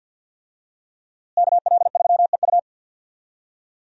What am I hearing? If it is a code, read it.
Morse code, heard as a keyed tone, about 51 wpm, 700 Hz. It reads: KL3EU
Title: USA Call Signs - 5 Characters - Repeated 3x - 50wpm